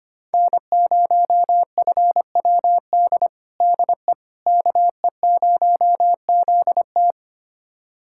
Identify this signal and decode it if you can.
Morse code, heard as a keyed tone, about 25 wpm, 700 Hz. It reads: N0FWD DE KE0ZT
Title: USA Callsign DE Callsign - 5 Characters - 25wpm